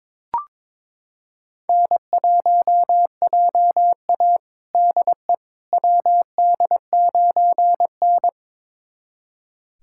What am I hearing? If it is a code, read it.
Morse code, heard as a keyed tone, about 22 wpm, 700 Hz. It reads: N1JA DE WD9N
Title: USA Callsign DE Callsign - 4 Characters - 22wpm